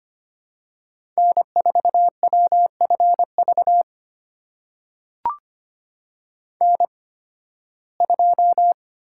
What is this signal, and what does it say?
Morse code, heard as a keyed tone, about 25 wpm, spaced as for 16 wpm, 700 Hz. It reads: N4WFV N2
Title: USA Call Signs - 5 Characters - 8x Character Spacing - 25wpm